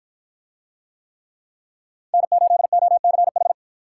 Morse code, heard as a keyed tone, about 51 wpm, 700 Hz. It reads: N8OCH